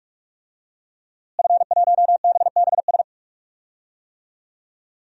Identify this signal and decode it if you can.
Morse code, heard as a keyed tone, about 46 wpm, 700 Hz. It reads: F1BBS
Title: DXCC-100 Call Signs - 5 Characters - Repeated 3x - 45wpm